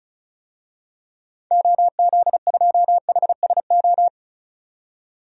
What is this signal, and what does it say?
Morse code, heard as a keyed tone, about 36 wpm, 700 Hz. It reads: OZ2HSO
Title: DXCC-100 Call Signs - 6 Characters - 35wpm